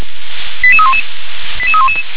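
Five-tone selective call (ZVEI), heard as quick runs of five tones, tones of about 70 ms. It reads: 88311
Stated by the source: ZVEI five-tone selective call 88311, received by SDR.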